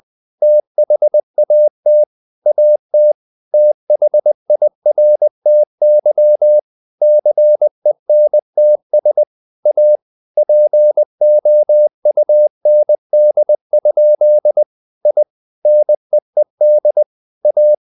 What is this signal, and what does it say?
Morse code, heard as a keyed tone, about 20 wpm, 600 Hz. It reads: THAT AT THIRTY CENTS A POUND? I NEED A
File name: the old man and the sea part 2 20wpm